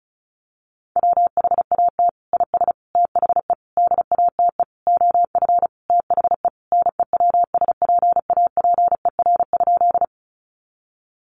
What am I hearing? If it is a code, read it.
Morse code, heard as a keyed tone, about 34 wpm, 700 Hz. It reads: WHAT IS THE DATE OF THE NEWSPAPER?
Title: Sentences from Top 1100 Words 35wpm